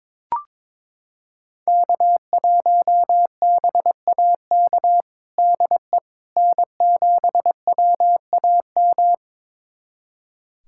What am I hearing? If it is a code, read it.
Morse code, heard as a keyed tone, about 22 wpm, 700 Hz. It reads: K1BAK DE N7WAM
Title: USA Callsign DE Callsign - 5 Characters - 22wpm